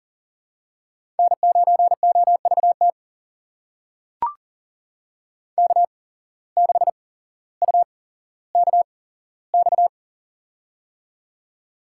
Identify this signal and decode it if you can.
Morse code, heard as a keyed tone, about 40 wpm, spaced as for 17 wpm, 700 Hz. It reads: N9OVT K6UKX
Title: USA Call Signs - 5 Characters - 8x Character Spacing - 40wpm